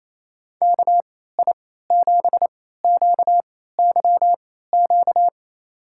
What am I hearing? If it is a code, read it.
Morse code, heard as a keyed tone, about 28 wpm, spaced as for 16 wpm, 700 Hz. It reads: KI7QYQ
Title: USA Call Signs - 3x Character Spacing - 28wpm